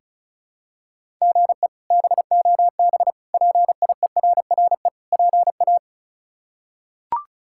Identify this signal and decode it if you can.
Morse code, heard as a keyed tone, about 36 wpm, 700 Hz. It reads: GE BOB PIERRE PA
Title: K1USN SST QSO Practice - 35wpm